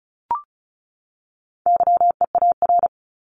Morse code, heard as a keyed tone, about 34 wpm, 700 Hz. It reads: YEAR
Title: ES:S Suffix - Top 100 Words with Base Words 35wpm